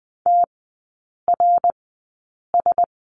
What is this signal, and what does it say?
Morse code, heard as a keyed tone, about 20 wpm, spaced as for 8 wpm, 700 Hz. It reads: TRS
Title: 067 Lesson 12e - Letter d - Three Characters 20wpm